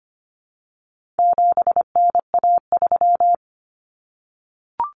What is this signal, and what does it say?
Morse code, heard as a keyed tone, about 25 wpm, 700 Hz. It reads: 7NA3
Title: Common Cut Numbers - 4-Digit 25wpm